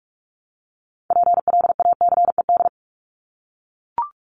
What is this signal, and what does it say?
Morse code, heard as a keyed tone, about 44 wpm, 700 Hz. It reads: PLACED
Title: ED Suffix - Top 100 Words with Base Words 45wpm